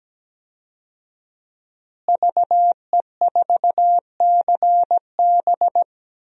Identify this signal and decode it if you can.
Morse code, heard as a keyed tone, about 17 wpm, 700 Hz. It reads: VE4CB
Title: DXCC-100 Call Signs - 5 Characters - Repeated 3x - 17wpm